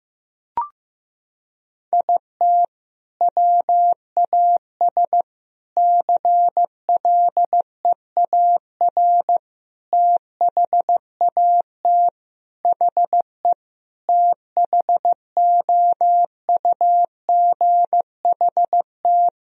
Morse code, heard as a keyed tone, about 16 wpm, 700 Hz. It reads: IT WAS CLEAR THAT HE THOUGHT
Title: Sentences from Top 1000 Words - Review Part 4 - 15wpm